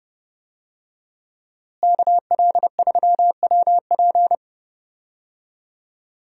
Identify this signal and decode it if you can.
Morse code, heard as a keyed tone, about 32 wpm, 700 Hz. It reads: KL3WP
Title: USA Call Signs - 5 Characters - Repeated 2x - 30wpm